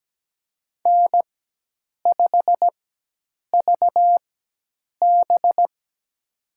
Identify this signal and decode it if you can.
Morse code, heard as a keyed tone, about 17 wpm, spaced as for 8 wpm, 700 Hz. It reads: N5VB
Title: USA Call Signs - 4 Characters - 4x Character Spacing - 17wpm